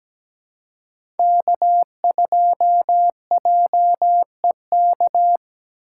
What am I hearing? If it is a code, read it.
Morse code, heard as a keyed tone, about 17 wpm, 700 Hz. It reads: K2JEK
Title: USA Call Signs - 3x Character Spacing - 17wpm